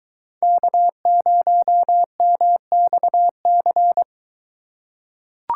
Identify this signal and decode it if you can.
Morse code, heard as a keyed tone, about 23 wpm, 700 Hz. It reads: K0MXC